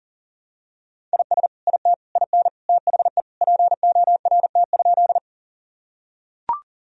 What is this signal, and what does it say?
Morse code, heard as a keyed tone, about 40 wpm, 700 Hz. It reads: IS IT IN THE PORT?
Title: Sentences from Top 1000 Words - Review Part 1 - 40wpm